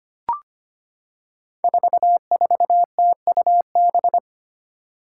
Morse code, heard as a keyed tone, about 25 wpm, 700 Hz. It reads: 44TUB
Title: Five Challenging Letter-Number Combinations 25wpm